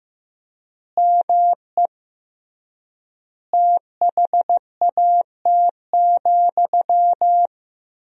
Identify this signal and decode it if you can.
Morse code, heard as a keyed tone, about 15 wpm, 700 Hz. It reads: ME THAT,
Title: 3x Word-Spacing - Sentences from Top 500 Words - 15wpm